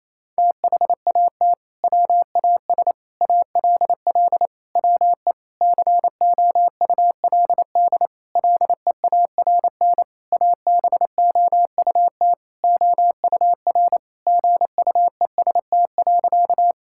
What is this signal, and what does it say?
Morse code, heard as a keyed tone, about 28 wpm, 700 Hz. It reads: THAT WAS ALL WE COULD LEARN ABOUT OUR GUEST.
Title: Sentences from Top 1600 Words 28wpm